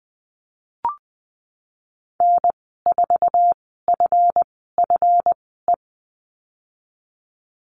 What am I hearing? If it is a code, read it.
Morse code, heard as a keyed tone, about 20 wpm, spaced as for 15 wpm, 700 Hz. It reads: N4FFE
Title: 154 Lesson 25g - Letter p - Call Signs 20wpm